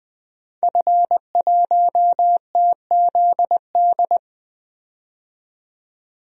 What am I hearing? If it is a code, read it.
Morse code, heard as a keyed tone, about 20 wpm, 700 Hz. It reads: F1TZD
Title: DXCC-100 Call Signs - 5 Characters - 20wpm